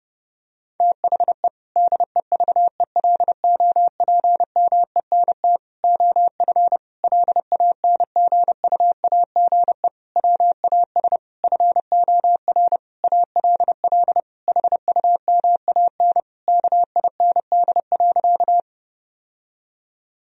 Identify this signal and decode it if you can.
Morse code, heard as a keyed tone, about 32 wpm, 700 Hz. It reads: THE DEVELOPMENT OF LANGUAGE WAS FOR ALL HUMAN KIND.
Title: Sentences from Top 1000 Words - Review Part 4 - 30wpm